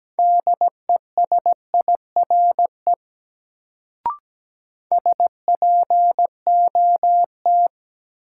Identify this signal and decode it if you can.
Morse code, heard as a keyed tone, about 17 wpm, 700 Hz. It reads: DESIRE SPOT
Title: Sentences from Top 1100 Words - New Word Drill - 17wpm